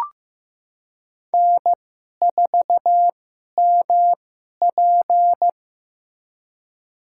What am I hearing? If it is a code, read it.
Morse code, heard as a keyed tone, about 16 wpm, spaced as for 11 wpm, 700 Hz. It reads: N4MP
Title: USA Call Signs - 4 Characters - 2x Character Spacing - 15wpm